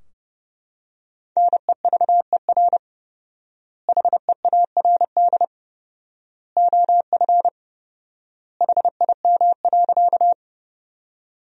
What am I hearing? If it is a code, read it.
Morse code, heard as a keyed tone, about 30 wpm, 700 Hz. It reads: NEVER HEARD OF HIM.
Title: 4x Word-Spacing - Sentences from Top 400 Words - 30wpm